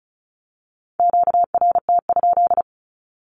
Morse code, heard as a keyed tone, about 36 wpm, 700 Hz. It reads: QRT?
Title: Q-Codes with Spoken Meaning - 35wpm